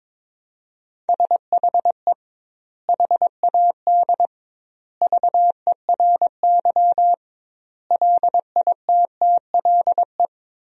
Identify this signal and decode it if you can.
Morse code, heard as a keyed tone, about 22 wpm, 700 Hz. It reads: SHE HAD VERY LITTLE
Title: 2x Word-Spacing - Sentences from Top 400 Words - 22wpm